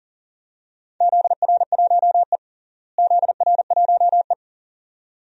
Morse code, heard as a keyed tone, about 40 wpm, 700 Hz. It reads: ZR1E ZR1E
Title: DXCC-100 Call Signs - 4 Characters - Repeated 2x - 40wpm